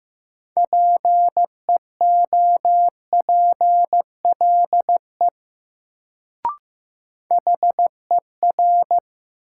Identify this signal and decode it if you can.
Morse code, heard as a keyed tone, about 15 wpm, 700 Hz. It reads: PEOPLE HER
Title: Sets of 2 Words - New Word Drill - 15wpm